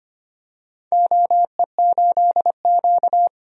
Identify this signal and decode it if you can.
Morse code, heard as a keyed tone, about 25 wpm, 700 Hz. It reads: OE8Q
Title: DXCC-100 Call Signs - 4 Characters - 25wpm